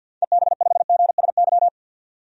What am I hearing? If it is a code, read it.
Morse code, heard as a keyed tone, about 53 wpm, 700 Hz. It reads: EB5GSY
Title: DXCC-100 Call Signs - 6 Characters - 50wpm